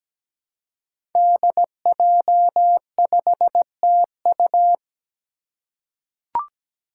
Morse code, heard as a keyed tone, about 17 wpm, 700 Hz. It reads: DJ5TU
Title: Five Challenging Letter-Number Combinations 17wpm